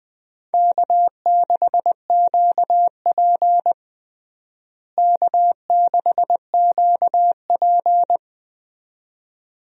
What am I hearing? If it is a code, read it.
Morse code, heard as a keyed tone, about 20 wpm, 700 Hz. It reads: K6QP K6QP
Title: USA Call Signs - 4 Characters - Repeated 2x - 20wpm